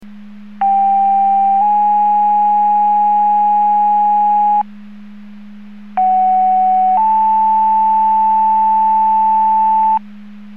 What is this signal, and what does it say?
Two-tone page, heard as a one-second tone, then a three-second tone, about 790 then 830 Hz, about 750 then 880 Hz. It is heard from 0.6 to 10.0 s.